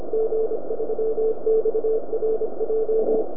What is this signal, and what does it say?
Morse code, heard as a keyed tone, about 26 wpm, 440 Hz, with radio noise. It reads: G3XRJ